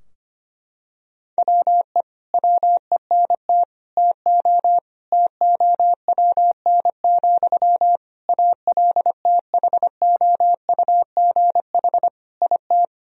Morse code, heard as a keyed tone, about 25 wpm, 700 Hz. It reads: WE WENT TO TOWN, ALTHOUGH IT
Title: Sentences from Top 1000 Words - Review Part 3 - 25wpm